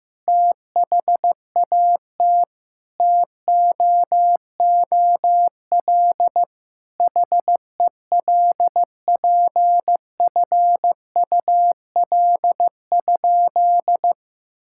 Morse code, heard as a keyed tone, about 15 wpm, 700 Hz. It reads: THAT TOOL HELPFUL?
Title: Sentences from Top 1000 Words - Review Part 6 - 15wpm